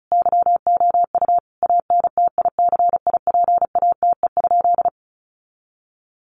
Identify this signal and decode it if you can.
Morse code, heard as a keyed tone, about 34 wpm, 700 Hz. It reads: YOU ANTICIPATE?
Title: Sentences from Top 1500 Words 35wpm